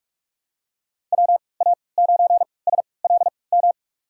Morse code, heard as a keyed tone, about 46 wpm, spaced as for 26 wpm, 700 Hz. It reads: WA9SLM
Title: USA Call Signs - 3x Character Spacing - 45wpm